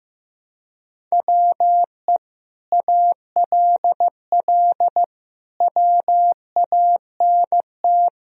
Morse code, heard as a keyed tone, about 15 wpm, 700 Hz. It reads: WE ALL WANT
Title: Sentences from Top 1000 Words - Review Part 4 - 15wpm